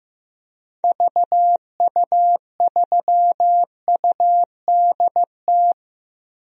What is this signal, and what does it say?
Morse code, heard as a keyed tone, about 15 wpm, 700 Hz. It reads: VU3UDT
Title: DXCC-100 Call Signs - 6 Characters - 15wpm